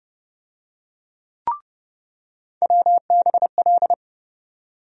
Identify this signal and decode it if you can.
Morse code, heard as a keyed tone, about 30 wpm, 700 Hz. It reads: WBL
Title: USA Call Sign Postfixes - 3 Letters 30wpm